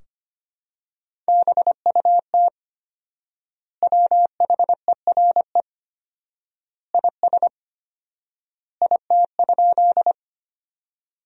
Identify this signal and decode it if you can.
Morse code, heard as a keyed tone, about 25 wpm, 700 Hz. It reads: BUT WHERE IS IT?
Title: 4x Word-Spacing - Sentences from Top 200 Words - 25wpm